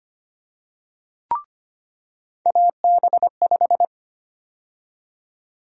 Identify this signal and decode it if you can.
Morse code, heard as a keyed tone, about 25 wpm, 700 Hz. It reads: AB5
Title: USA Call Sign Prefixes - 25wpm